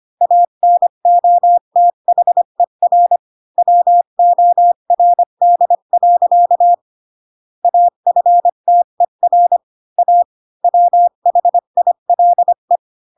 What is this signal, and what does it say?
Morse code, heard as a keyed tone, about 25 wpm, 700 Hz. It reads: ANOTHER WORD. AFTER A WHILE
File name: Pinocchio / Pinocchio ch8 20 25 ik1hge